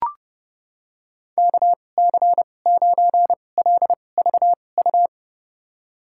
Morse code, heard as a keyed tone, about 30 wpm, spaced as for 22 wpm, 700 Hz. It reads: KC9LVU
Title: USA Call Signs - 6 Characters - 2x Character Spacing - 30wpm